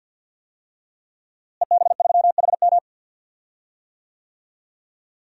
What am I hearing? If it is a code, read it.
Morse code, heard as a keyed tone, about 49 wpm, 700 Hz. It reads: EB3HM
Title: DXCC-100 Call Signs - 5 Characters - Repeated 2x - 50wpm